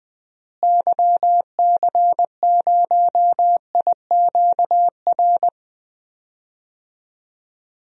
Morse code, heard as a keyed tone, about 20 wpm, 700 Hz. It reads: YC0IQR
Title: DXCC-100 Call Signs - 6 Characters - Repeated 2x - 20wpm